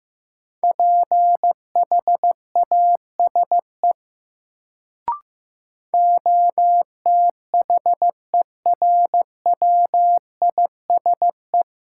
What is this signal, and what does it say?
Morse code, heard as a keyed tone, about 16 wpm, 700 Hz. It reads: PHASE OTHERWISE
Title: Sentences from Top 1200 Words - New Word Drill - 15wpm